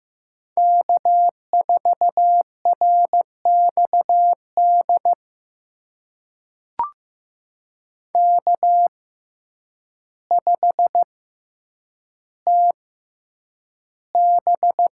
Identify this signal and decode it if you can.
Morse code, heard as a keyed tone, about 15 wpm, spaced as for 8 wpm, 700 Hz. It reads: K4RXD K5TB
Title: USA Call Signs - 5 Characters - 6x Character Spacing - 15wpm